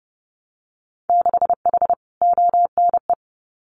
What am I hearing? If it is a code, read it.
Morse code, heard as a keyed tone, about 30 wpm, 700 Hz. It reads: 6H ONE
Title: Winter Field Day Exchanges Encore 30wpm